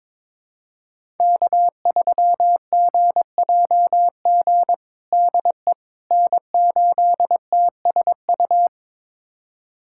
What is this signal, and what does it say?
Morse code, heard as a keyed tone, about 22 wpm, 700 Hz. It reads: K3GJG DE N8TSU
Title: USA Callsign DE Callsign - 5 Characters - 22wpm